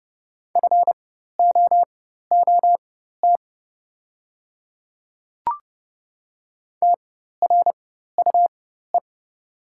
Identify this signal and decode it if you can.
Morse code, heard as a keyed tone, about 30 wpm, spaced as for 14 wpm, 690 Hz. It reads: FOOT TRUE
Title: 4x Character Spacing - Top 500 Words - 30wpm